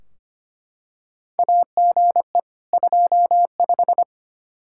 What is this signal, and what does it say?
Morse code, heard as a keyed tone, about 25 wpm, 700 Hz. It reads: AGE 25
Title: Practice QSO - Exchange 3 - More Info - 25wpm